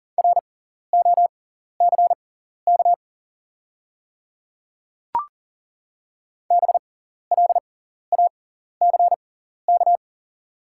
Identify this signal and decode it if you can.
Morse code, heard as a keyed tone, about 40 wpm, spaced as for 14 wpm, 700 Hz. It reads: ROCK BLACK
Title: 6x Character Spacing - Top 500 Words - 40wpm